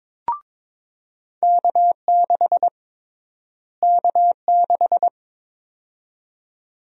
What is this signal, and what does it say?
Morse code, heard as a keyed tone, about 22 wpm, 700 Hz. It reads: K6 K6
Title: USA Call Sign Prefixes - Repeated 2x - 22wpm